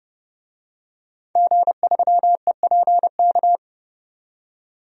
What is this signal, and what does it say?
Morse code, heard as a keyed tone, about 30 wpm, 700 Hz. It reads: G3EPK
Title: DXCC-100 Call Signs - 5 Characters - 30wpm